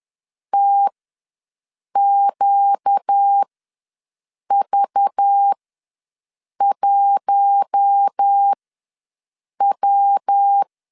Morse code, heard as a keyed tone, about 10 wpm, spaced as for 6 wpm, 800 Hz. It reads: TQV1W
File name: LEZIONI / 6 PKX0 / 6 PKX0 ws 10wpm 4220